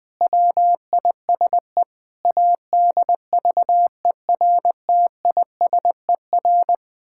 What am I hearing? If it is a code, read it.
Morse code, heard as a keyed tone, about 20 wpm, 700 Hz. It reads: WISE ADVERTISER
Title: Pangram Challenge v2 - 20wpm